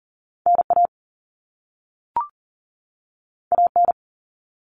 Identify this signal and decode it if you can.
Morse code, heard as a keyed tone, about 42 wpm, 700 Hz. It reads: NA AN
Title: All Cut Numbers - 2-Digit 40wpm